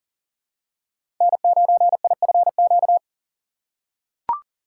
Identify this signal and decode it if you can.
Morse code, heard as a keyed tone, about 40 wpm, 700 Hz. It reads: N9IFQ
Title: USA Call Signs - 5 Characters - 3x Character Spacing - 40wpm